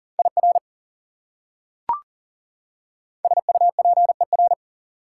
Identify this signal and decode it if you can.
Morse code, heard as a keyed tone, about 40 wpm, 700 Hz. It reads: IR SUPER
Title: Prefixes and Suffixes 40wpm